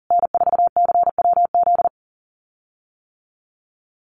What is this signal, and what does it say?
Morse code, heard as a keyed tone, about 40 wpm, 700 Hz. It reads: N4CWZ
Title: USA Call Signs Repeated 3x 40wpm